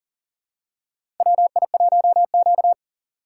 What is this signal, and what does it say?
Morse code, heard as a keyed tone, about 40 wpm, 700 Hz. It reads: WI1Q
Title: USA Call Signs - 4 Characters - 6x Character Spacing - 40wpm